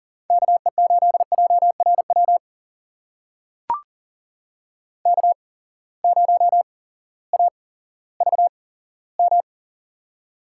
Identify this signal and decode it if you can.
Morse code, heard as a keyed tone, about 40 wpm, spaced as for 18 wpm, 700 Hz. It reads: KE8JRW K0AVM
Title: USA Call Signs - 8x Character Spacing - 40wpm